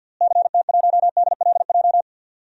Morse code, heard as a keyed tone, about 52 wpm, 700 Hz. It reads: CT1DLJ
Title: DXCC-100 Call Signs - 6 Characters - Repeated 3x - 50wpm